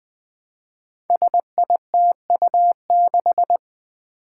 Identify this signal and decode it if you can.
Morse code, heard as a keyed tone, about 20 wpm, 700 Hz. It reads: SITU6
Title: Five Challenging Letter-Number Combinations 20wpm